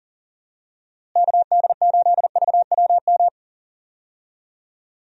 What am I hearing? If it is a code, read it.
Morse code, heard as a keyed tone, about 37 wpm, 700 Hz. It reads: KD8VWM
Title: USA Call Signs - 6 Characters - Repeated 2x - 40wpm